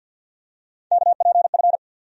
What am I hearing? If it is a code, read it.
Morse code, heard as a keyed tone, about 52 wpm, 700 Hz. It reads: KPF
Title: USA Call Sign Postfixes - 3 Letters 50wpm